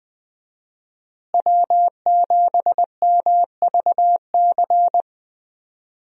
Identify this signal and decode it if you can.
Morse code, heard as a keyed tone, about 20 wpm, 700 Hz. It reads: W7MVC
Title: USA Call Signs - 5 Characters - Repeated 2x - 20wpm